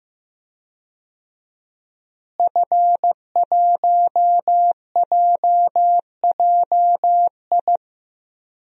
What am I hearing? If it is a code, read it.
Morse code, heard as a keyed tone, about 16 wpm, 700 Hz. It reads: F1JJI
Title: DXCC-100 Call Signs - 5 Characters - Repeated 3x - 15wpm